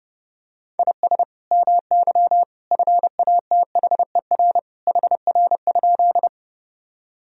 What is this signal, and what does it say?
Morse code, heard as a keyed tone, about 31 wpm, 700 Hz. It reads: IS MY FATHER HR?
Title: Sentences from Top 300 Words and Abbreviations 30wpm